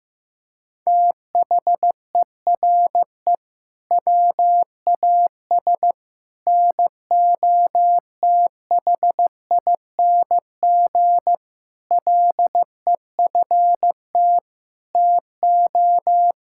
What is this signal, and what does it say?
Morse code, heard as a keyed tone, about 15 wpm, 700 Hz. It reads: THERE WAS NOTHING LEFT TO
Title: Sentences from Top 1000 Words - Review Part 6 - 15wpm